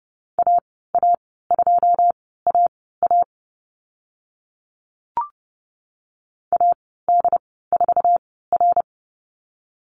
Morse code, heard as a keyed tone, about 30 wpm, spaced as for 17 wpm, 700 Hz. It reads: AA2AA AD4R
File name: Lesson 14g - Number 2 - Call Signs 30wpm